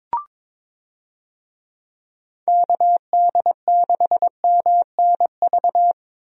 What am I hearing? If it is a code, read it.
Morse code, heard as a keyed tone, about 22 wpm, 700 Hz. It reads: KD6MNV
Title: USA Call Signs - 6 Characters - 22wpm